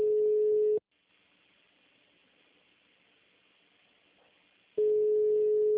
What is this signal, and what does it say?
Ringback tone (Europe, 425 Hz).